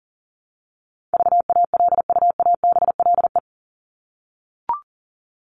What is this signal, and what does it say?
Morse code, heard as a keyed tone, about 40 wpm, 700 Hz. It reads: VALUABLE
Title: Top 1001-1250 Words 40wpm